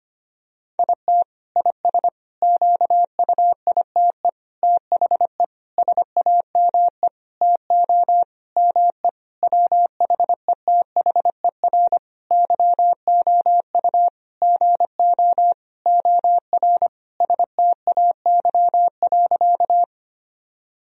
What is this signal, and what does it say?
Morse code, heard as a keyed tone, about 25 wpm, 700 Hz. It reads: IT IS QUITE THE SAME TO ME WHETHER YOU GO OR STAY.
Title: Sentences from Top 1000 Words - Review Part 3 - 25wpm